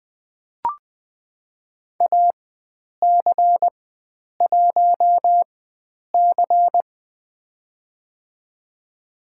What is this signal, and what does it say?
Morse code, heard as a keyed tone, about 20 wpm, spaced as for 9 wpm, 700 Hz. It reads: AC1C
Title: USA Call Signs - 4 Characters - 4x Character Spacing - 20wpm